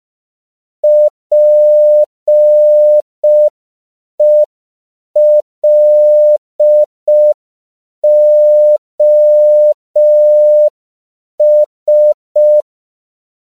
Morse code, heard as a keyed tone, about 5 wpm, 600 Hz. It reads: PELOS
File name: CW / 5PPM